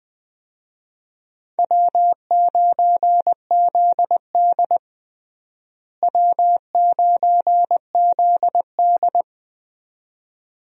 Morse code, heard as a keyed tone, about 21 wpm, 700 Hz. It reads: W9ZD W9ZD
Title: USA Call Signs - 4 Characters - Repeated 2x - 20wpm